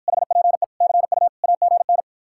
Morse code, heard as a keyed tone, about 54 wpm, 700 Hz. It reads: HPE CU AGN